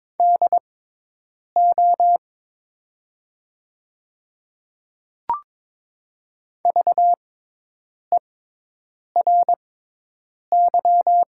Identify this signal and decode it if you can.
Morse code, heard as a keyed tone, about 22 wpm, spaced as for 8 wpm, 700 Hz. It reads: DO VERY
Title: 6x Character Spacing - Top 500 Words - 22wpm